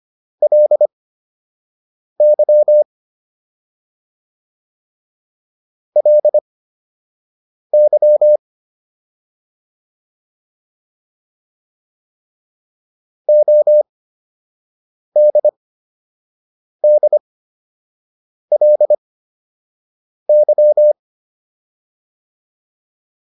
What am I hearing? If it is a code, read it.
Morse code, heard as a keyed tone, about 25 wpm, spaced as for 6 wpm, 600 Hz. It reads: LY LY ODDLY